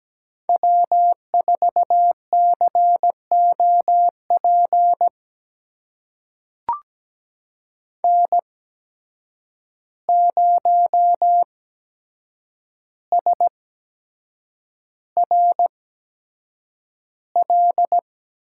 Morse code, heard as a keyed tone, about 17 wpm, spaced as for 7 wpm, 700 Hz. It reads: W4COP N0SRL
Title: USA Call Signs - 5 Characters - 8x Character Spacing - 17wpm